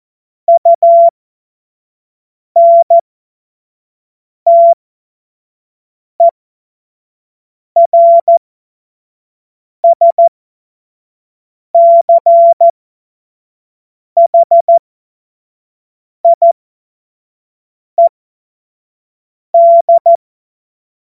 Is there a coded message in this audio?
Yes: Morse code from 0.5 to 20.2 s, heard as a keyed tone, about 14 wpm, spaced as for 5 wpm, 680 Hz. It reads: UNTERSCHIED